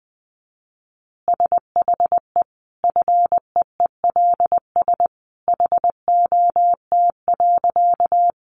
Morse code, heard as a keyed tone, about 20 wpm, 700 Hz. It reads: SHE FEELS HOT.
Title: Additional Easy Phrases 20wpm